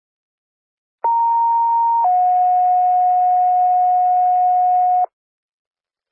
Two-tone page, heard as a one-second tone, then a three-second tone, about 930 then 730 Hz. It is heard from 1.0 to 5.0 s.